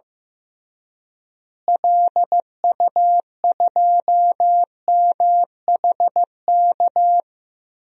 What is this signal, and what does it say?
Morse code, heard as a keyed tone, about 15 wpm, 700 Hz. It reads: LU2MHK
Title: DXCC-100 Call Signs - 6 Characters - Repeated 3x - 15wpm